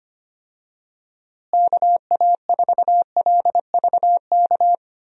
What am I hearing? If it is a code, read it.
Morse code, heard as a keyed tone, about 25 wpm, 700 Hz. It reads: KA4LVK
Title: USA Call Signs - 6 Characters - Repeated 3x - 25wpm